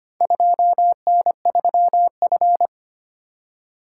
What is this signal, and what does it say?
Morse code, heard as a keyed tone, about 25 wpm, 700 Hz. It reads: 2N3F
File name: practice / 25wpm / Four Letter-Number Combinations - 25wpm